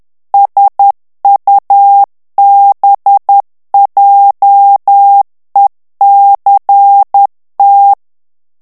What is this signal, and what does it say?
Morse code, heard as a keyed tone, about 10 wpm, 800 Hz. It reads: SUBJECT
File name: ANGLAIS texte clair 3 vitesse 10 mots minute